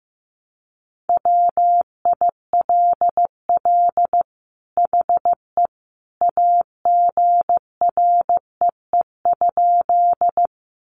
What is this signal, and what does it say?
Morse code, heard as a keyed tone, about 15 wpm, 700 Hz. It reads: WILL HE AGREE?
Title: Sentences from Top 1100 Words 15wpm